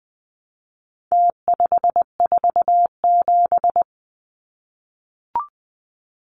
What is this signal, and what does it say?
Morse code, heard as a keyed tone, about 20 wpm, 700 Hz. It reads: T547
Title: Numbers - Contest Style - 4-Digit 20wpm